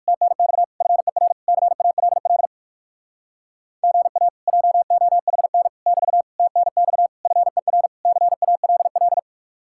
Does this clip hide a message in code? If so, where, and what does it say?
Yes: Morse code from 0.1 to 9.2 s, heard as a keyed tone, about 59 wpm, 700 Hz. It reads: TNX FER CALL GA JOHN <BT> TNX FER CALL